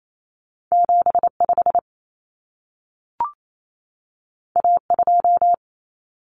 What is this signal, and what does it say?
Morse code, heard as a keyed tone, about 28 wpm, 700 Hz. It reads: 75 A2
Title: Common Cut Numbers - 2-Digit 28wpm